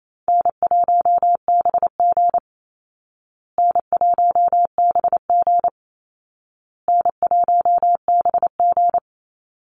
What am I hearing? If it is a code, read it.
Morse code, heard as a keyed tone, about 28 wpm, 700 Hz. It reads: N1BG N1BG N1BG
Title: Easy Call Signs Repeated 3x - 28wpm